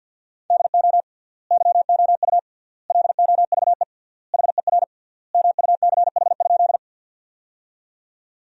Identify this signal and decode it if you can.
Morse code, heard as a keyed tone, about 52 wpm, 700 Hz. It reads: DO YOU LOVE HER MUCH?
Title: 3x Word-Spacing - Sentences from Top 400 Words - 50wpm